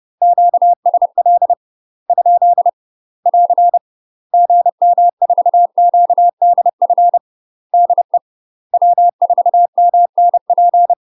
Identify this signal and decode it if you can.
Morse code, heard as a keyed tone, about 30 wpm, 700 Hz. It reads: QSL ? <AR> GM4QDF DE W4MNP
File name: UK MASTER QSOs TRAINING / 30WPM / UK MASTER QSOs 0040